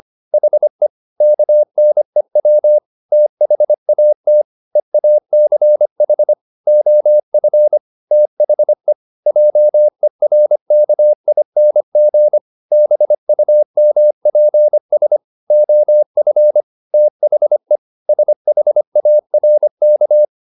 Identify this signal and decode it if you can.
Morse code, heard as a keyed tone, about 25 wpm, 600 Hz. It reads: HE KNEW THAT EACH OF THE JERKING BUMPS OF THE SHARK